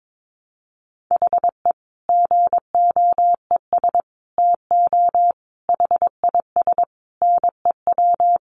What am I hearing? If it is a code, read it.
Morse code, heard as a keyed tone, about 22 wpm, 700 Hz. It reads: HE GOES TO HIS NEW
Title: Easy Phrases 22wpm